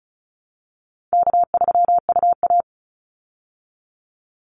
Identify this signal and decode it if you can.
Morse code, heard as a keyed tone, about 34 wpm, 700 Hz. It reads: K3UA
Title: Call Signs Repeated 3x - 35wpm